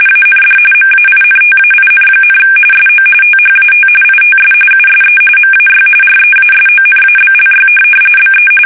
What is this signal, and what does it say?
RTTY, about 45.45 baud. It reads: AND MAINTAIN, SO THAT ITS USE MAY REASONABLY BE
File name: fsk850